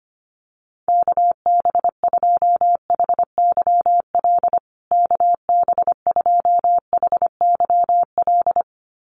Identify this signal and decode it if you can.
Morse code, heard as a keyed tone, about 25 wpm, 700 Hz. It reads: KB2HYL KB2HYL
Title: USA Call Signs Repeated 2x 25wpm